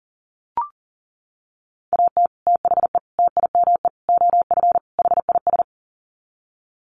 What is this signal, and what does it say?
Morse code, heard as a keyed tone, about 40 wpm, 700 Hz. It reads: AT THE TIME OF HIS
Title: Sets of 5 Words Encore2 40wpm